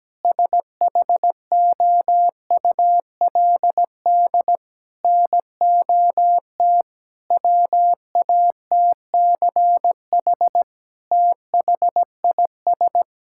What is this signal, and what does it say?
Morse code, heard as a keyed tone, about 17 wpm, 700 Hz. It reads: SHOULD NOT WATCH THIS
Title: Sentences from Top 1000 Words - Review Part 5 - 17wpm